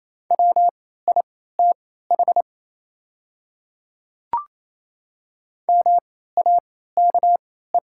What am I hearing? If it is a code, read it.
Morse code, heard as a keyed tone, about 28 wpm, spaced as for 16 wpm, 700 Hz. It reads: WITH MAKE
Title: 3x Character Spacing - Top 100 Words - 28wpm